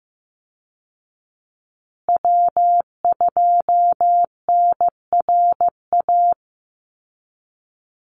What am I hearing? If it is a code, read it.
Morse code, heard as a keyed tone, about 15 wpm, 700 Hz. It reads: W2NRA
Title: Call Signs Repeated 2x - 15wpm